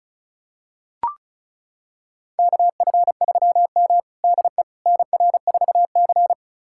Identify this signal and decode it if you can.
Morse code, heard as a keyed tone, about 34 wpm, 700 Hz. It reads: KF3M DE NR4C
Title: USA Callsign DE Callsign - 4 Characters - 35wpm